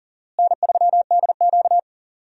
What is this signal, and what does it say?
Morse code, heard as a keyed tone, about 37 wpm, 700 Hz. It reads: N3DQ
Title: USA Call Signs - 4 Characters - Repeated 2x - 40wpm